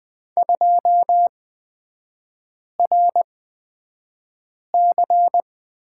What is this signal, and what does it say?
Morse code, heard as a keyed tone, about 20 wpm, spaced as for 5 wpm, 700 Hz. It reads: 2RC
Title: 224 Lesson 36e - BT - Three Characters 20wpm